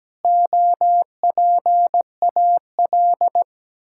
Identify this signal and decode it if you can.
Morse code, heard as a keyed tone, about 17 wpm, 700 Hz. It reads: OPAL
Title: Pangram Challenge v2 - 17wpm